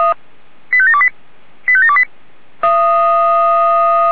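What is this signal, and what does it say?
Five-tone selective call (ZVEI), heard as quick runs of five tones, tones of about 70 ms. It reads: 85728 85728
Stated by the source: ZVEI five-tone call received by SDR.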